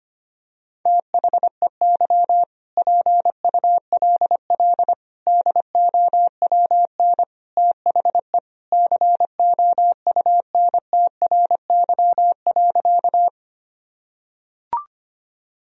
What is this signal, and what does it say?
Morse code, heard as a keyed tone, about 25 wpm, 700 Hz. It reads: THEY PULL DOWN THE COUNTRY.
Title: Sentences from Top 1000 Words - Review Part 2 - 25wpm